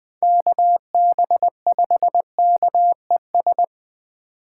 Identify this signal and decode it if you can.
Morse code, heard as a keyed tone, about 24 wpm, 700 Hz. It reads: KB5KES
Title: Speed Racing - USA 2x3 Call Signs 20wpm